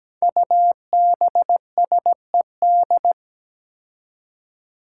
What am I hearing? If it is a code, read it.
Morse code, heard as a keyed tone, about 17 wpm, 700 Hz. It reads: UBSED
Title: Five Challenging Letter-Number Combinations 17wpm